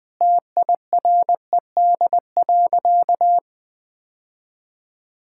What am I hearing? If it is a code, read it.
Morse code, heard as a keyed tone, about 20 wpm, 700 Hz. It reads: TIRED.
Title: Sentences with 5-Word Phrases - 20wpm